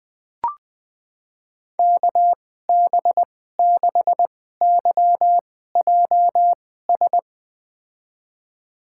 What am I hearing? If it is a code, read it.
Morse code, heard as a keyed tone, about 20 wpm, spaced as for 14 wpm, 700 Hz. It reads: KB6YJS
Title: USA Call Signs - 6 Characters - 2x Character Spacing - 20wpm